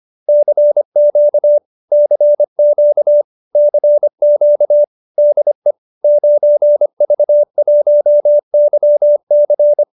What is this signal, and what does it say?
Morse code, heard as a keyed tone, about 25 wpm, 590 Hz. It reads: CQ CQ CQ DE 9V1YC
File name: qso207b 25